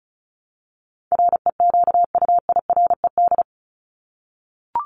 Morse code, heard as a keyed tone, about 34 wpm, 700 Hz. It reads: REQUIRED
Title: ED Suffix - Top 100 Words with Base Words 35wpm